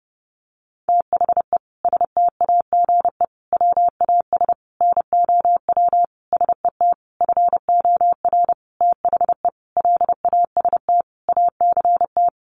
Morse code, heard as a keyed tone, about 30 wpm, 700 Hz. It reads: THE STAGE WAS NOW SET FOR THE LAST ACT
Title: Sentences from Top 1500 Words 30wpm